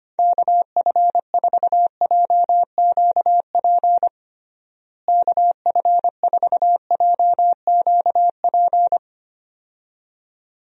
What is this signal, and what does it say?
Morse code, heard as a keyed tone, about 25 wpm, 700 Hz. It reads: KF4JQP KF4JQP
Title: USA Call Signs - 6 Characters - Repeated 2x - 25wpm